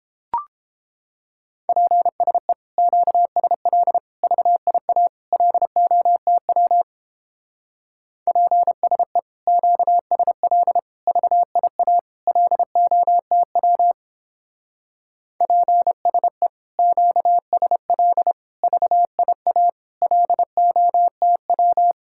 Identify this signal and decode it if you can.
Morse code, heard as a keyed tone, about 29 wpm, 700 Hz. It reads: PSE QSL VIA LOTW PSE QSL VIA LOTW PSE QSL VIA LOTW